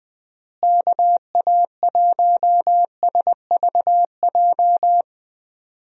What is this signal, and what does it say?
Morse code, heard as a keyed tone, about 20 wpm, 700 Hz. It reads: KA1SVJ
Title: USA Call Signs - 6 Characters - Repeated 2x - 20wpm